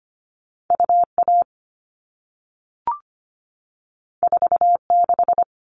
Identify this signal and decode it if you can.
Morse code, heard as a keyed tone, about 25 wpm, 700 Hz. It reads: UA 46
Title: All Cut Numbers - 2-Digit 25wpm